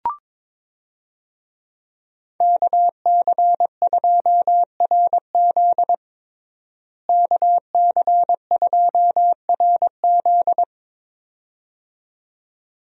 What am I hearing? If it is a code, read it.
Morse code, heard as a keyed tone, about 22 wpm, 700 Hz. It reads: KC2RZ KC2RZ
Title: USA Call Signs - 5 Characters - Repeated 2x - 22wpm